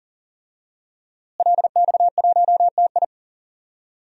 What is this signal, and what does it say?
Morse code, heard as a keyed tone, about 40 wpm, 700 Hz. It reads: LX1TI
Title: DXCC-100 Call Signs - 5 Characters - Repeated 3x - 40wpm